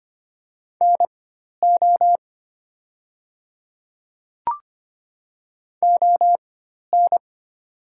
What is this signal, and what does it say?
Morse code, heard as a keyed tone, about 25 wpm, spaced as for 12 wpm, 700 Hz. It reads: NO ON
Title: 4x Character Spacing - Top 100 Words - 25wpm